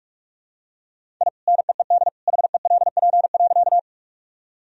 Morse code, heard as a keyed tone, about 46 wpm, 700 Hz. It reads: I NEED HELP.
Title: Sentences from Top 1000 Words - Review Part 1 - 45wpm